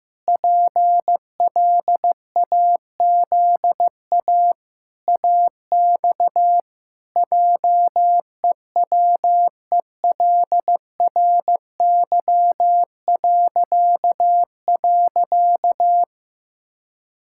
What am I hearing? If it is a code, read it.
Morse code, heard as a keyed tone, about 15 wpm, 700 Hz. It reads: PLAZA AX JEWELRY..
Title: Pangram Challenge v2 - 15wpm